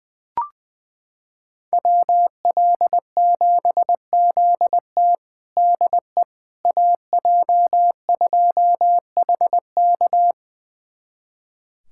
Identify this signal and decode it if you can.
Morse code, heard as a keyed tone, about 20 wpm, 700 Hz. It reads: WL7ZT DE AJ2HK
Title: USA Callsign DE Callsign - 5 Characters - 20wpm